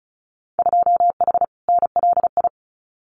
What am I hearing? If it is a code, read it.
Morse code, heard as a keyed tone, about 34 wpm, 700 Hz. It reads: 2H NLI
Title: Winter Field Day Exchanges Encore 35wpm